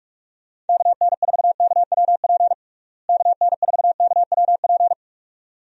Morse code, heard as a keyed tone, about 46 wpm, 700 Hz. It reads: KN4KWP KN4KWP
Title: USA Call Signs - 6 Characters - Repeated 2x - 45wpm